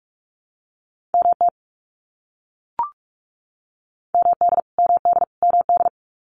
Morse code, heard as a keyed tone, about 44 wpm, 700 Hz. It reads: MT MD MD MD
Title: US States Repeated 3x 45wpm